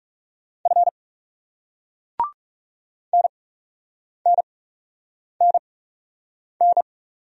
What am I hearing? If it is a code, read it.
Morse code, heard as a keyed tone, about 40 wpm, spaced as for 8 wpm, 700 Hz. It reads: F NNNN